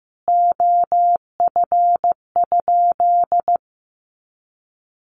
Morse code, heard as a keyed tone, about 15 wpm, 700 Hz. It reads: OF?
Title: Sentences from Top 200 Words 15wpm